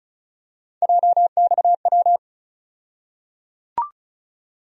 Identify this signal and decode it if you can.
Morse code, heard as a keyed tone, about 36 wpm, 700 Hz. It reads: JXW
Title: USA Call Sign Postfixes - 3 Letters 35wpm